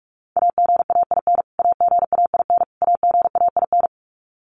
Morse code, heard as a keyed tone, about 44 wpm, 700 Hz. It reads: AGAIN AGAIN AGAIN
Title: Top 300 Words Repeated 45wpm